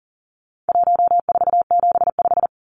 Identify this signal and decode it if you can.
Morse code, heard as a keyed tone, about 40 wpm, 700 Hz. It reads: 1475
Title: Numbers - 4-Digit 40wpm